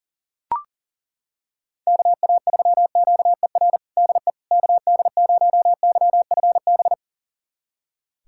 Morse code, heard as a keyed tone, about 40 wpm, 700 Hz. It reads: KA3QER DE KD0YFB
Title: USA Callsign DE Callsign - 6 Characters - 40wpm